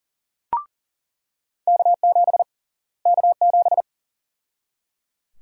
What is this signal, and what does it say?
Morse code, heard as a keyed tone, about 37 wpm, 700 Hz. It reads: K7 K7
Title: USA Call Sign Prefixes - Repeated 2x - 40wpm